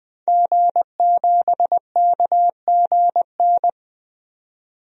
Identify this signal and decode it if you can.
Morse code, heard as a keyed tone, about 20 wpm, 700 Hz. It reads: G7KGN
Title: DXCC-100 Call Signs - 5 Characters - 20wpm